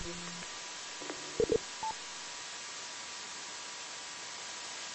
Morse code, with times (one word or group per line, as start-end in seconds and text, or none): none